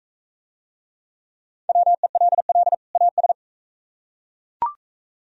Morse code, heard as a keyed tone, about 44 wpm, 690 Hz. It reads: WELL AS